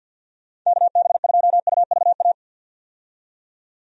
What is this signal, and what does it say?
Morse code, heard as a keyed tone, about 51 wpm, 700 Hz. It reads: KB2UVA
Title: USA Call Signs - 6 Characters - 50wpm